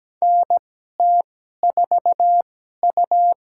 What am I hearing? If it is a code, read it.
Morse code, heard as a keyed tone, about 17 wpm, spaced as for 12 wpm, 700 Hz. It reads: NT4U
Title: USA Call Signs - 4 Characters - 2x Character Spacing - 17wpm